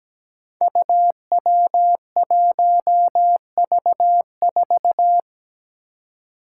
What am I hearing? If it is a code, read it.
Morse code, heard as a keyed tone, about 17 wpm, 700 Hz. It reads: UW1V4
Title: Five Challenging Letter-Number Combinations 17wpm